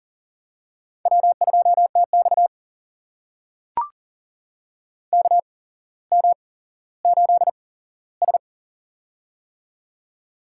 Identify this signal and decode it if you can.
Morse code, heard as a keyed tone, about 42 wpm, spaced as for 17 wpm, 700 Hz. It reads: W2TX KM8S
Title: USA Call Signs - 4 Characters - 8x Character Spacing - 40wpm